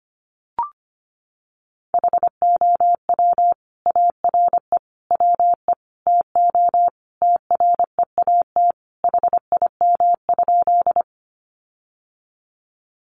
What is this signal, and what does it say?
Morse code, heard as a keyed tone, about 25 wpm, 700 Hz. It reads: HOW ARE WE TO TREAT HIM?
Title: Sentences from Top 1300 Words 25wpm